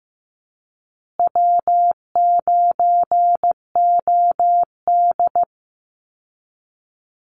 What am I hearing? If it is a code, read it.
Morse code, heard as a keyed tone, about 15 wpm, 700 Hz. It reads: W9OD
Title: Call Signs Repeated 2x - 15wpm